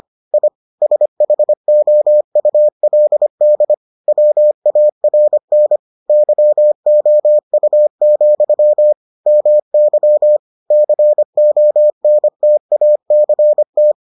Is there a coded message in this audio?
Yes: Morse code from 0.3 to 13.9 s, heard as a keyed tone, about 25 wpm, 600 Hz. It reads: I SHOULD WARN YOU, MY CONTACT